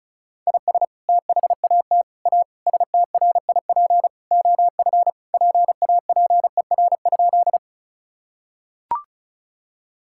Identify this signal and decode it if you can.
Morse code, heard as a keyed tone, about 34 wpm, 700 Hz. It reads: IS THAT A STRIP OF PAPER?
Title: Sentences from Top 1600 Words 35wpm